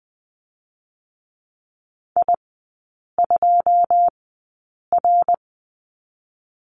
Morse code, heard as a keyed tone, about 20 wpm, spaced as for 8 wpm, 700 Hz. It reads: I2R
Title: 127 Lesson 21e - Number 6 - Three Characters 20wpm